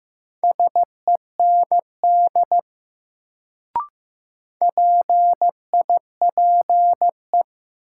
Morse code, heard as a keyed tone, about 15 wpm, 700 Hz. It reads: SEND PIPE
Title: Sentences from Top 1500 Words - New Word Drill - 15wpm